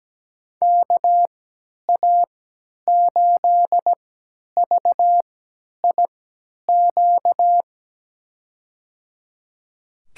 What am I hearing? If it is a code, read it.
Morse code, heard as a keyed tone, about 17 wpm, spaced as for 10 wpm, 700 Hz. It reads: KA8VIQ
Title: USA Call Signs - 6 Characters - 3x Character Spacing - 17wpm